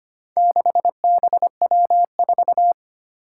Morse code, heard as a keyed tone, about 25 wpm, 700 Hz. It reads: 6BW4
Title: Four Challenging Letter-Number Combinations 25wpm